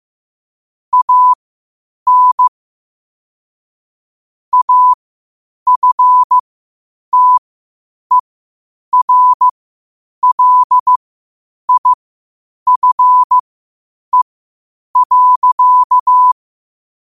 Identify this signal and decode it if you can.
Morse code, heard as a keyed tone, about 15 wpm, spaced as for 8 wpm, 1000 Hz. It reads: AN AFTERLIFE.
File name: qotdfiles / 8wpm